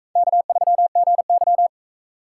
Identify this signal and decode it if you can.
Morse code, heard as a keyed tone, about 41 wpm, 700 Hz. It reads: K3GY